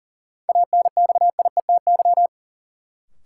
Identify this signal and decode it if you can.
Morse code, heard as a keyed tone, about 40 wpm, 700 Hz. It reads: ANXIETY
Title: Sentences from Top 1600 Words - New Word Drill Alt - 40wpm